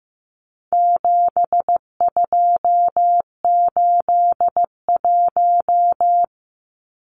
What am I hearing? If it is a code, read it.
Morse code, heard as a keyed tone, about 15 wpm, 700 Hz. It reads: 7281
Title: Numbers - 4-Digit 15wpm